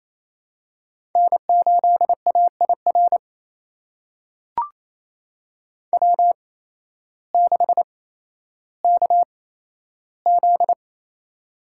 Morse code, heard as a keyed tone, about 28 wpm, spaced as for 13 wpm, 700 Hz. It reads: N8AIR W6KZ
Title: USA Call Signs - 5 Characters - 8x Character Spacing - 28wpm